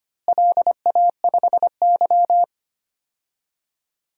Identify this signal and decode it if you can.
Morse code, heard as a keyed tone, about 25 wpm, 700 Hz. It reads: LA5Y
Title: DXCC-100 Call Signs - 4 Characters - Repeated 2x - 25wpm